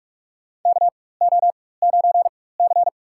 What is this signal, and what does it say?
Morse code, heard as a keyed tone, about 46 wpm, spaced as for 21 wpm, 700 Hz. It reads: KO9C
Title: USA Call Signs - 4x Character Spacing - 45wpm